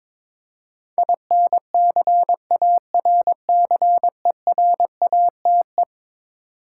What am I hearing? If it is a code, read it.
Morse code, heard as a keyed tone, about 22 wpm, 700 Hz. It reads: INCARCERATE
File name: practice / 22wpm / Eleven Letter Words - 22wpm